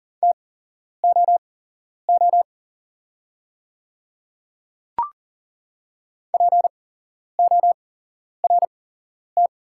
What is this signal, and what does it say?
Morse code, heard as a keyed tone, about 40 wpm, spaced as for 11 wpm, 690 Hz. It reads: TOO PORT
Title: 8x Character Spacing - Top 200 Words - 40wpm